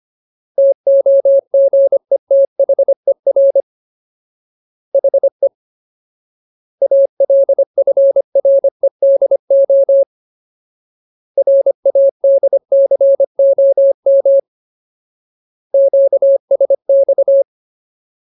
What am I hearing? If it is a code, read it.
Morse code, heard as a keyed tone, about 25 wpm, 550 Hz. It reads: TOGETHER HE ALFREDO RADCOM QSX
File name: Ultimate Word List / 25WPM Ultimate